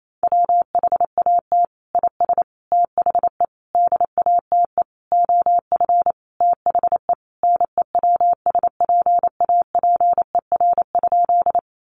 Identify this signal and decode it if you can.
Morse code, heard as a keyed tone, about 28 wpm, 700 Hz. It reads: WHAT IS THE DATE OF THE NEWSPAPER?
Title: Sentences from Top 1100 Words 28wpm